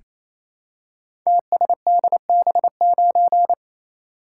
Morse code, heard as a keyed tone, about 27 wpm, 700 Hz. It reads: TSDB9
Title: Code Groups - 5 Letter:Numbers - 28wpm